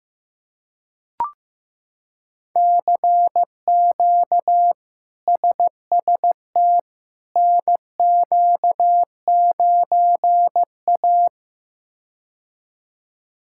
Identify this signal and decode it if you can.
Morse code, heard as a keyed tone, about 16 wpm, 700 Hz. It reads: CQ SST NQ9A
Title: K1USN SST QSO Practice - 15wpm